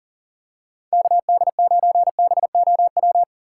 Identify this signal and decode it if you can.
Morse code, heard as a keyed tone, about 40 wpm, 700 Hz. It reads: KD9BOW
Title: USA Call Signs - 6 Characters - 4x Character Spacing - 40wpm